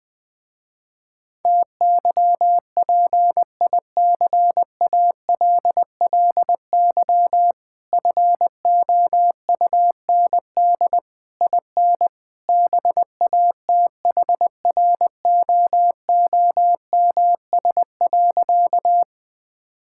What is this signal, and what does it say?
Morse code, heard as a keyed tone, about 20 wpm, 700 Hz. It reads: TYPICALLY FOUND IN BATHROOMS.
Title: Difficult Words - Speed-Racing with Hints - 20wpm